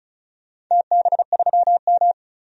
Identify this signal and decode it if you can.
Morse code, heard as a keyed tone, about 36 wpm, 700 Hz. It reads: TB3M
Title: DXCC-100 Call Signs - 4 Characters - 35wpm